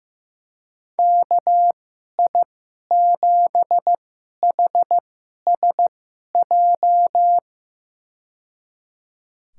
Morse code, heard as a keyed tone, about 15 wpm, spaced as for 11 wpm, 700 Hz. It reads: KI7HSJ
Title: USA Call Signs - 6 Characters - 2x Character Spacing - 15wpm